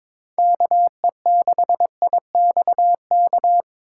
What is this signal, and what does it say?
Morse code, heard as a keyed tone, about 24 wpm, 700 Hz. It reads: KE6IXK
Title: Speed Racing - USA 2x3 Call Signs 22wpm